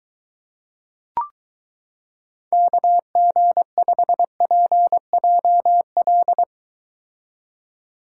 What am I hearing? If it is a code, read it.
Morse code, heard as a keyed tone, about 23 wpm, 700 Hz. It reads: KG5PJL